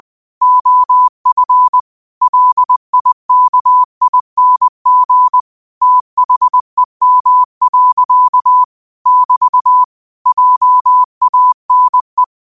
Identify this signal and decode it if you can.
Morse code, heard as a keyed tone, about 20 wpm, 1000 Hz. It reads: OF LIKING THEM. <BT> JANE
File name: qotdfiles / 20wpm